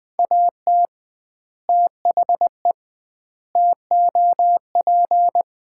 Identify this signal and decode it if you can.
Morse code, heard as a keyed tone, about 20 wpm, 700 Hz. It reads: AT THE TOP
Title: 2x Word-Spacing - Sets of 4 Words - 20wpm